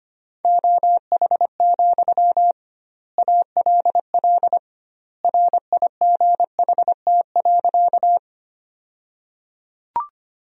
Morse code, heard as a keyed tone, about 25 wpm, 700 Hz. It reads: OH, ALL RIGHT.
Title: 2x Word-Spacing - Sentences from Top 500 Words - 25wpm